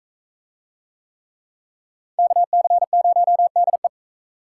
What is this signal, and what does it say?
Morse code, heard as a keyed tone, about 41 wpm, 700 Hz. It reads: KC0DE